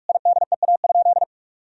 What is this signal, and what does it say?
Morse code, heard as a keyed tone, about 45 wpm, 700 Hz. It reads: IDEA?
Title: Sentences from Top 400 Words and Abbreviations 45wpm